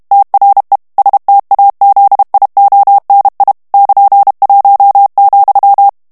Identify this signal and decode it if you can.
Morse code, heard as a keyed tone, about 32 wpm, 800 Hz. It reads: TRE STAZIONI <KN>1,
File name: ITALIEN texte clair 2 vitesse 30 mots minute